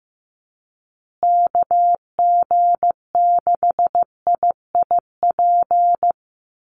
Morse code, heard as a keyed tone, about 15 wpm, 700 Hz. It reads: KG6IIP
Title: Call Signs Repeated 2x - 15wpm